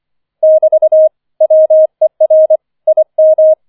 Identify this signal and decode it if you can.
Morse code, heard as a keyed tone, about 25 wpm, 630 Hz. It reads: <BT> WER IM
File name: 005 Fünfter Streich 25wpm